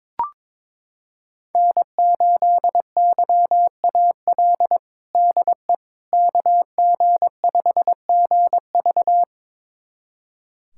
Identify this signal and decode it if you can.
Morse code, heard as a keyed tone, about 22 wpm, 700 Hz. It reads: N8YAL DE KG5GV
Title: USA Callsign DE Callsign - 5 Characters - 22wpm